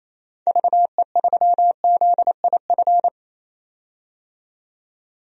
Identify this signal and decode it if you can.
Morse code, heard as a keyed tone, about 28 wpm, 700 Hz. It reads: VE3ZIF
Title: DXCC-100 Call Signs - 6 Characters - 28wpm